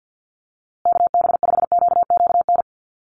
Morse code, heard as a keyed tone, about 50 wpm, 700 Hz. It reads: KB5QQN
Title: USA Call Signs Repeated 3x 50wpm